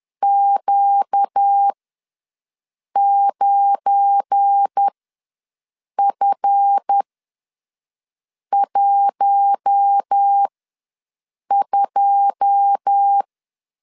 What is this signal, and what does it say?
Morse code, heard as a keyed tone, about 10 wpm, spaced as for 5 wpm, 800 Hz. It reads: Q9F12
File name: LEZIONI / 10 SIGNS / 10B INT ws 10wpm 4134